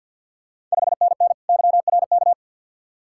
Morse code, heard as a keyed tone, about 50 wpm, 690 Hz. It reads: 5NN YRK
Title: Ontario QSO Party Exchange 50wpm